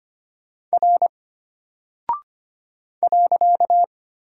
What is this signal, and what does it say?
Morse code, heard as a keyed tone, about 25 wpm, 700 Hz. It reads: R .
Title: Single Character (Letter:Number:Punctuation:Prosign) Alt 25wpm